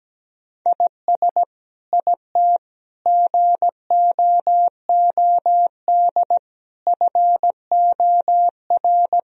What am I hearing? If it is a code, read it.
Morse code, heard as a keyed tone, about 17 wpm, 700 Hz. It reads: IS IT GOOD FOR
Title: Sentences from Top 1000 Words - Review Part 4 - 17wpm